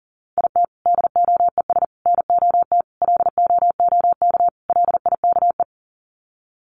Morse code, heard as a keyed tone, about 40 wpm, 700 Hz. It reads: IT DOES NOT LOOK LIKE
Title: Sets of 5 Words Encore2 40wpm